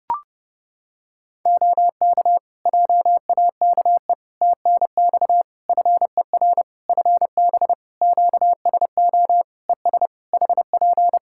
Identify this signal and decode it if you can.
Morse code, heard as a keyed tone, about 30 wpm, 700 Hz. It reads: OK JAKE TNX FER FB QSO ES HP
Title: Practice QSO - Exchange 4 - Ending - 30wpm